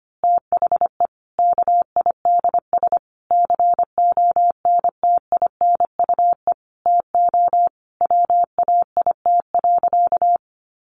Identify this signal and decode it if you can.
Morse code, heard as a keyed tone, about 25 wpm, 700 Hz. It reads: THE KIDS CONTINUE TO WAIT.
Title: Easy Phrases 25wpm